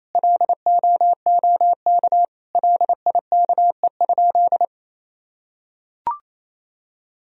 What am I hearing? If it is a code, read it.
Morse code, heard as a keyed tone, about 28 wpm, 700 Hz. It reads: LOOK LIKE?
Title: Sentences from Top 100 Words and Abbreviations 28wpm